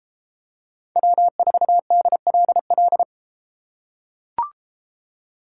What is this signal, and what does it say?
Morse code, heard as a keyed tone, about 30 wpm, 700 Hz. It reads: W4DLL